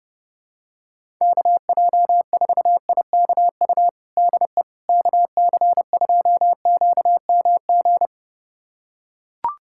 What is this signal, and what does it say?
Morse code, heard as a keyed tone, about 30 wpm, 700 Hz. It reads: KJ4IKU DE KC2QMG
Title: USA Callsign DE Callsign - 6 Characters - 30wpm